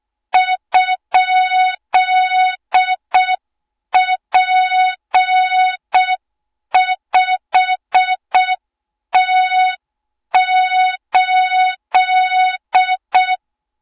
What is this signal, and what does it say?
Morse code, heard as a keyed tone, about 6 wpm, 3700 Hz. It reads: ?P5T8
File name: Morze-HA2MN / hang / 20 GYAKORLAT